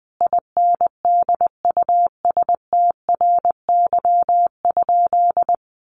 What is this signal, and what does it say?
Morse code, heard as a keyed tone, about 20 wpm, 700 Hz. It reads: INDUSTRY?
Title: Sentences from Top 1300 Words 20wpm